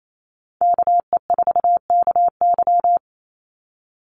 Morse code, heard as a keyed tone, about 28 wpm, 700 Hz. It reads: KE4KY
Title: Call Signs Repeated 2x - 28wpm